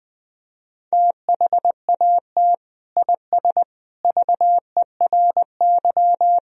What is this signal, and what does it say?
Morse code, heard as a keyed tone, about 20 wpm, 700 Hz. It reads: THAT IS VERY
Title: Sentences from Top 1000 Words - Review Part 2 - 20wpm